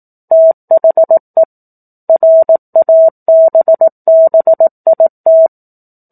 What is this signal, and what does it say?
Morse code, heard as a keyed tone, about 18 wpm, 650 Hz. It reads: THE RABBIT
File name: 04-Chapter-04 18 WPM